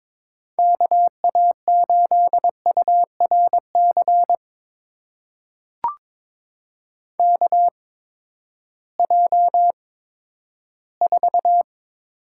Morse code, heard as a keyed tone, about 23 wpm, 700 Hz. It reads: K A 8 U R C K J 4
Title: USA Call Signs - 6 Characters - 8x Character Spacing - 22wpm